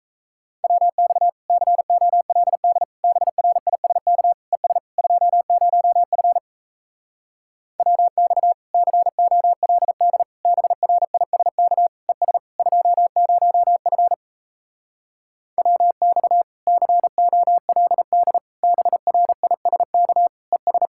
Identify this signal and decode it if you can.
Morse code, heard as a keyed tone, about 37 wpm, 700 Hz. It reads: WX COLD BRISK ES 20F WX COLD BRISK ES 20F WX COLD BRISK ES